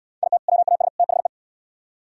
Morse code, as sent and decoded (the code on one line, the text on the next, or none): .. - / .-- .- ... / - .... .
IT WAS THE